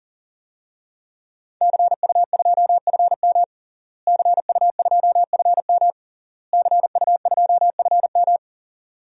Morse code, heard as a keyed tone, about 40 wpm, 700 Hz. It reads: CU2FM CU2FM CU2FM
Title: DXCC-100 Call Signs - 5 Characters - Repeated 3x - 40wpm